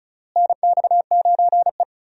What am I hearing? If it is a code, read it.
Morse code, heard as a keyed tone, about 36 wpm, 700 Hz. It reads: NX9E
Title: USA Call Signs - 4 Characters - Repeated 3x - 35wpm